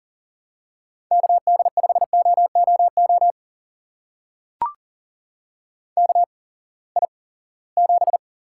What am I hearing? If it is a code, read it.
Morse code, heard as a keyed tone, about 39 wpm, spaced as for 22 wpm, 700 Hz. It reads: KD5OOO KI7
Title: USA Call Signs - 6 Characters - 8x Character Spacing - 40wpm